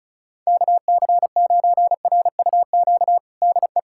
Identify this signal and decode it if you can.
Morse code, heard as a keyed tone, about 36 wpm, 700 Hz. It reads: KC9RUQ DE
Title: USA Callsign DE Callsign - 6 Characters - 35wpm